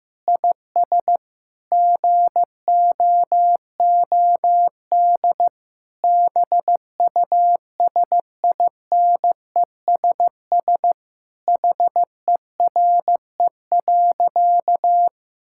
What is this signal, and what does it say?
Morse code, heard as a keyed tone, about 15 wpm, 700 Hz. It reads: IS GOOD BUSINESS HERE.
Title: Sentences from Top 1000 Words - Review Part 5 - 15wpm